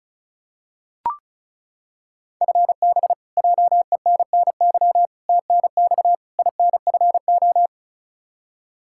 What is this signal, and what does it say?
Morse code, heard as a keyed tone, about 36 wpm, 700 Hz. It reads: FB JENNY TNX INFO
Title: Practice QSO - Exchange 3 - More Info - 35wpm